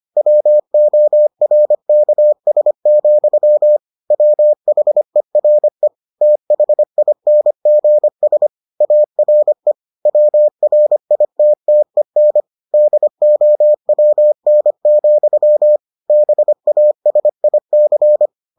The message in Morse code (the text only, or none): WORKS, WHERE THINGS ARE WRITTEN DOWN, BASIC